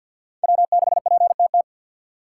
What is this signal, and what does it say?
Morse code, heard as a keyed tone, about 52 wpm, 700 Hz. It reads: W6PTT
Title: USA Call Signs - 5 Characters - 3x Character Spacing - 50wpm